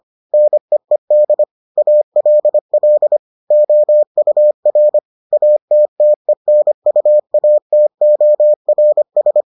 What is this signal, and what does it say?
Morse code, heard as a keyed tone, about 25 wpm, 600 Hz. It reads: NEED ALL OUR ATTENUATORS